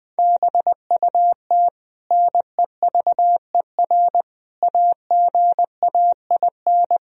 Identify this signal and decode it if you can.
Morse code, heard as a keyed tone, about 20 wpm, 700 Hz. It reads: BUT NEVER AGAIN
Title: Sentences from Top 1000 Words - Review Part 1 - 20wpm